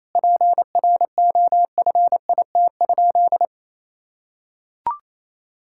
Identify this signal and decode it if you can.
Morse code, heard as a keyed tone, about 28 wpm, 700 Hz. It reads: PROFIT?
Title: Sentences from Top 1000 Words - Review Part 5 - 28wpm